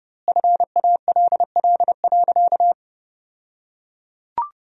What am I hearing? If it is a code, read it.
Morse code, heard as a keyed tone, about 30 wpm, 700 Hz. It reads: FALL.
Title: 2x Word-Spacing - Sentences from Top 500 Words - 30wpm